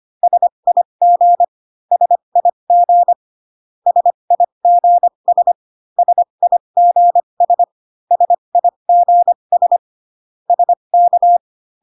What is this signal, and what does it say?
Morse code, heard as a keyed tone, about 25 wpm, spaced as for 22 wpm, 700 Hz. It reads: SIG SIG SIGS SIGS SIGS SK
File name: CW common abbreviations alphabetical repeated 3 times 25 18wpm